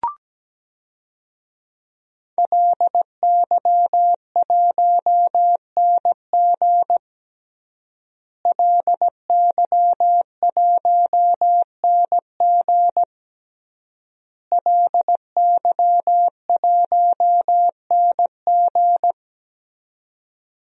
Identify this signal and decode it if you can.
Morse code, heard as a keyed tone, about 17 wpm, 700 Hz. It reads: LY1NG LY1NG LY1NG
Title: DXCC-100 Call Signs - 5 Characters - Repeated 3x - 17wpm